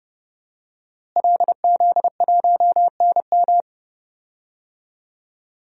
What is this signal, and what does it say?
Morse code, heard as a keyed tone, about 32 wpm, 700 Hz. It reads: LZ1NM
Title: DXCC-100 Call Signs - 5 Characters - Repeated 3x - 30wpm